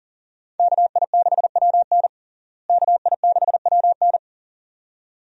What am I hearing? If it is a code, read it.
Morse code, heard as a keyed tone, about 40 wpm, 700 Hz. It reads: KI6WN KI6WN
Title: USA Call Signs - 5 Characters - Repeated 2x - 40wpm